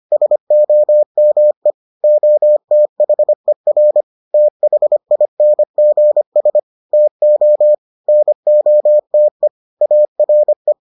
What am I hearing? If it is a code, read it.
Morse code, heard as a keyed tone, about 25 wpm, 600 Hz. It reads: SOME OTHER THINGS TO NOTE ARE